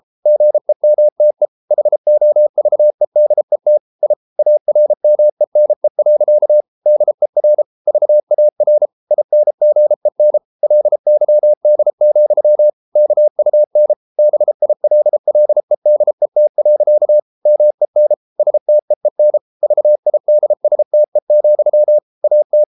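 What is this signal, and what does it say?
Morse code, heard as a keyed tone, about 33 wpm, 600 Hz. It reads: GEMTE HOVEDET I ARMENE. DER VAR INGEN LYD, KUN BILLEDET. MEN STEEN VIDSTE, AT
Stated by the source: Morse code at 33 wpm, QRQ.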